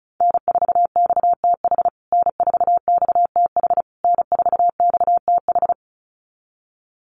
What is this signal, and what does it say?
Morse code, heard as a keyed tone, about 34 wpm, 700 Hz. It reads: N4XTH N4XTH N4XTH
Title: USA Call Signs Repeated 3x 35wpm